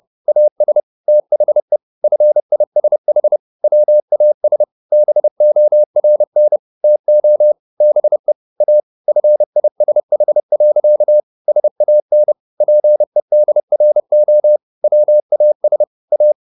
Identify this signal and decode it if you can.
Morse code, heard as a keyed tone, about 30 wpm, 600 Hz. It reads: AS THE FISH WAS BORN TO BE A FISH. SAN PEDRO WAS A